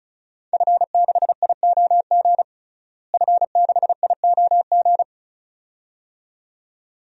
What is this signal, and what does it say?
Morse code, heard as a keyed tone, about 34 wpm, 700 Hz. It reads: F6IOG F6IOG
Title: DXCC-100 Call Signs - 5 Characters - Repeated 2x - 35wpm